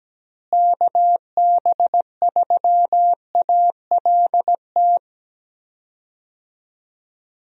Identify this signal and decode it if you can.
Morse code, heard as a keyed tone, about 17 wpm, 700 Hz. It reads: KB3ALT
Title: USA Call Signs - 6 Characters - Repeated 3x - 17wpm